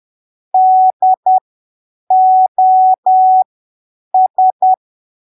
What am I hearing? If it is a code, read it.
Morse code, heard as a keyed tone, about 10 wpm, spaced as for 7 wpm, 750 Hz. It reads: DOS
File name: arquivos / texto / texto 13 10ppm